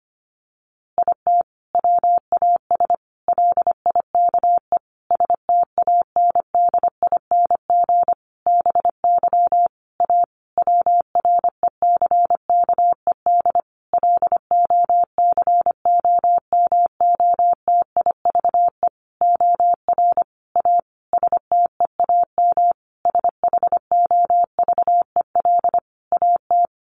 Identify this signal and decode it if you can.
Morse code, heard as a keyed tone, about 25 wpm, 700 Hz. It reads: IT WAS LIKE STANDING BY A WRECKED LOCOMOTIVE OR A STEAM SHOVEL AT
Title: A Sound Of Thunder 25wpm